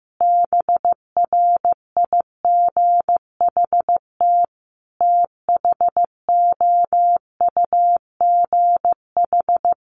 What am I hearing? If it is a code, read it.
Morse code, heard as a keyed tone, about 15 wpm, 700 Hz. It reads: BRIGHT THOUGH
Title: Sentences from Top 1000 Words 15wpm